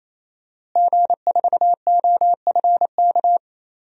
Morse code, heard as a keyed tone, about 28 wpm, 700 Hz. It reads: G4OFK
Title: DXCC-100 Call Signs - 5 Characters - Repeated 3x - 28wpm